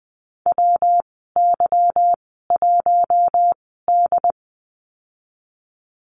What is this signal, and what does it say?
Morse code, heard as a keyed tone, about 20 wpm, spaced as for 15 wpm, 700 Hz. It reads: WY1D
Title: 161 Lesson 26g - Letter g - Call Signs 20wpm